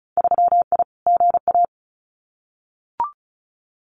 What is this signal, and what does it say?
Morse code, heard as a keyed tone, about 36 wpm, 700 Hz. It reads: 3I GA
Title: Winter Field Day Exchanges 35wpm